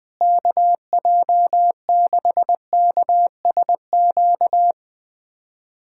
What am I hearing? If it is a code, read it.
Morse code, heard as a keyed tone, about 20 wpm, 700 Hz. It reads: KJ6KSQ
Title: USA Callsign DE Callsign 20wpm